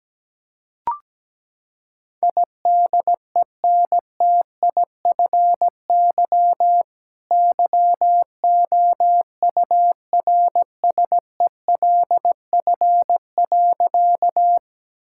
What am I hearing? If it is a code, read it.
Morse code, heard as a keyed tone, about 17 wpm, 700 Hz. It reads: IDENTIFY YOURSELF.
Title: Sentences from Top 1000 Words - Review Part 6 - 17wpm